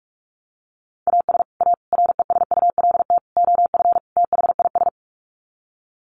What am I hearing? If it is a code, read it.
Morse code, heard as a keyed tone, about 44 wpm, 700 Hz. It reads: AS A RESULT OF THIS
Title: Sets of 5 Words 45wpm